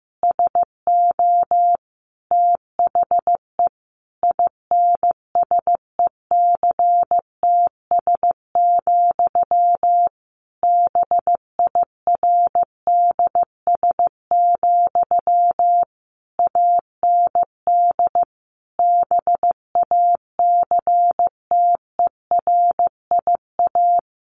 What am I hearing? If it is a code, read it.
Morse code, heard as a keyed tone, about 14 wpm, 700 Hz. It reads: SO THE INSECTS, BIRDS, AND BACTERIA
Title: A Sound Of Thunder 15wpm